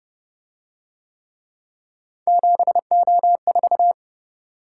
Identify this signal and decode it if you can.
Morse code, heard as a keyed tone, about 29 wpm, 700 Hz. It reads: 7O4